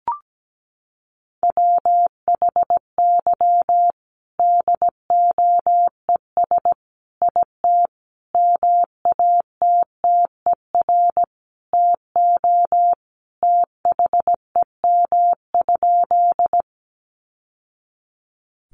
Morse code, heard as a keyed tone, about 17 wpm, 700 Hz. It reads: WHY DOES IT MATTER TO THEM?
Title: Sentences from Top 700 Words 17wpm